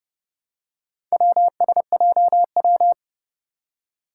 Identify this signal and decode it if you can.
Morse code, heard as a keyed tone, about 32 wpm, 700 Hz. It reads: WSJW
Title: Four Challenging Letter-Number Combinations 30wpm